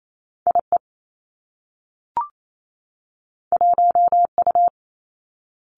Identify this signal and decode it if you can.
Morse code, heard as a keyed tone, about 28 wpm, 700 Hz. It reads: I E 1 U
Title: Two Challenging Letter-Number Combinations 28wpm